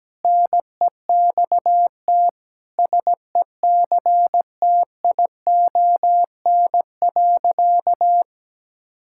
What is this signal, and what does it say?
Morse code, heard as a keyed tone, about 17 wpm, 700 Hz. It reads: NEXT SECTION.
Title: Sentences from Top 1000 Words - Review Part 5 - 17wpm